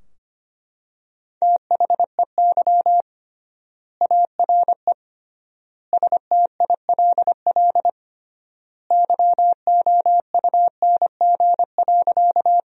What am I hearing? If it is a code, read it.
Morse code, heard as a keyed tone, about 25 wpm, 700 Hz. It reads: THEY ARE STILL YOUNG.
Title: 3x Word-Spacing - Sentences from Top 400 Words - 25wpm